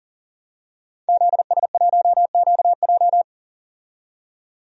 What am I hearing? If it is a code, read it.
Morse code, heard as a keyed tone, about 40 wpm, 700 Hz. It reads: ZS1QJ
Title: DXCC-100 Call Signs - 5 Characters - 40wpm